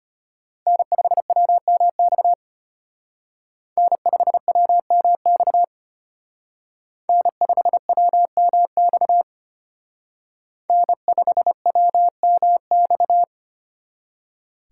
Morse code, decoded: N5WMX N5WMX N5WMX N5WMX

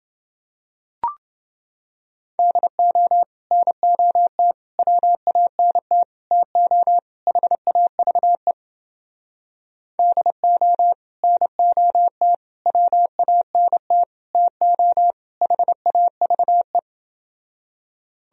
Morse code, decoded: DO NOT WANT TO HAVE DO NOT WANT TO HAVE

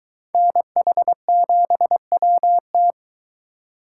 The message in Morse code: NH7WT

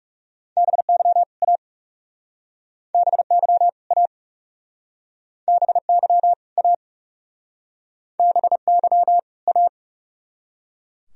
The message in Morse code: BY A BY A BY A BY A

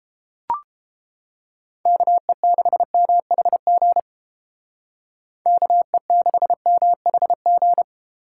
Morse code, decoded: KE6MHG KE6MHG

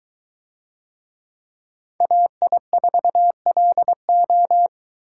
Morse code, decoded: AI4LO

